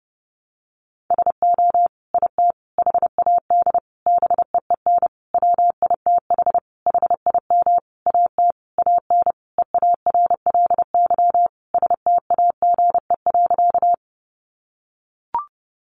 Morse code, heard as a keyed tone, about 30 wpm, 700 Hz. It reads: SO IT HAD BEEN WITH HIM AT AN EARLY STAGE.
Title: Sentences from Top 1200 Words 30wpm